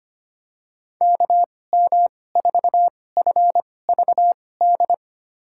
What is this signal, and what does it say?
Morse code, heard as a keyed tone, about 25 wpm, spaced as for 18 wpm, 700 Hz. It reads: KM4FVD
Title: USA Call Signs - 6 Characters - 2x Character Spacing - 25wpm